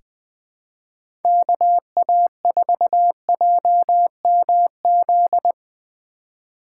Morse code, heard as a keyed tone, about 21 wpm, 700 Hz. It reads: KA4JMZ
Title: USA Call Signs - 6 Characters - 3x Character Spacing - 20wpm